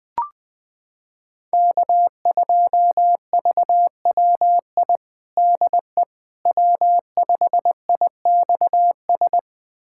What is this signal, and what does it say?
Morse code, heard as a keyed tone, about 20 wpm, 700 Hz. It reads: K2VWI DE W5IXS
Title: USA Callsign DE Callsign - 5 Characters - 20wpm